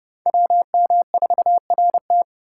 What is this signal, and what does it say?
Morse code, heard as a keyed tone, about 30 wpm, 700 Hz. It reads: WM4RT